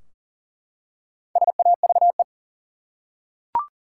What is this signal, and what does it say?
Morse code, heard as a keyed tone, about 40 wpm, 700 Hz. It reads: SAVE
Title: Four Letter Words - 40wpm